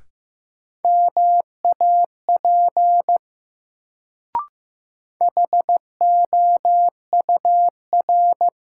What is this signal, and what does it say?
Morse code, heard as a keyed tone, about 15 wpm, 700 Hz. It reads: MAP HOUR
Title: Sentences from Top 400 Words - New Word Drill - 15wpm